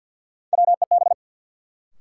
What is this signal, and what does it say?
Morse code, heard as a keyed tone, about 52 wpm, 700 Hz. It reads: WEB
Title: Sentences from Top 1000 Words - Review Part 4 - New Word Drill - 50wpm